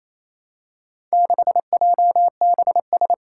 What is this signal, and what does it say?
Morse code, heard as a keyed tone, about 28 wpm, 700 Hz. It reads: 6JBS
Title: Four Challenging Letter-Number Combinations 28wpm